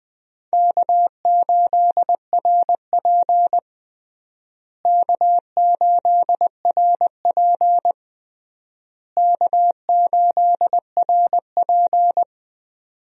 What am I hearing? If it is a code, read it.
Morse code, heard as a keyed tone, about 21 wpm, 700 Hz. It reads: K8RP K8RP K8RP
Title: USA Call Signs - 4 Characters - Repeated 3x - 20wpm